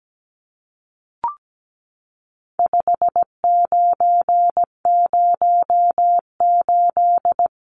Morse code, heard as a keyed tone, about 17 wpm, 700 Hz. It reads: 5908
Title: Numbers - 4-Digit 17wpm